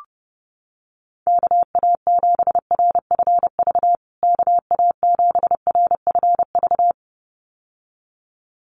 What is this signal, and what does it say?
Morse code, heard as a keyed tone, about 30 wpm, 700 Hz. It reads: KA7RFV KA7RFV
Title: USA Call Signs Repeated 2x 30wpm